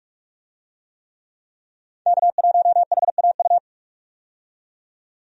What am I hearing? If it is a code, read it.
Morse code, heard as a keyed tone, about 42 wpm, 700 Hz. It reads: K1HAU
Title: USA Call Signs - 5 Characters - 45wpm